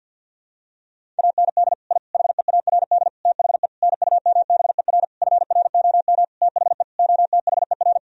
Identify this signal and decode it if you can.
Morse code, heard as a keyed tone, about 52 wpm, 700 Hz. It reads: AND I HEARD THE NUMBER FROM THE OTHER
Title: Sentences from Top 1000 Words - Review Part 1 - 50wpm